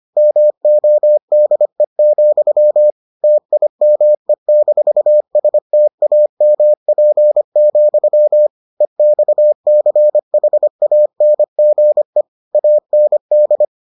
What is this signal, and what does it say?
Morse code, heard as a keyed tone, about 25 wpm, 600 Hz. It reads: MODE, TIME-STAMP, EXCHANGE AND